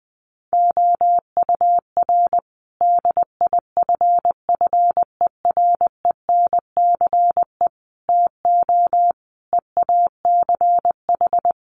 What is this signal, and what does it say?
Morse code, heard as a keyed tone, about 19 wpm, 700 Hz. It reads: OUR DIFFERENCE TO EACH
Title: Sentences from Top 1500 Words 20wpm